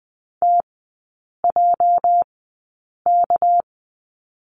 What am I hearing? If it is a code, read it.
Morse code, heard as a keyed tone, about 20 wpm, spaced as for 8 wpm, 700 Hz. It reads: TJK
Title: 205 Lesson 33e - Letter j - Three Characters 20wpm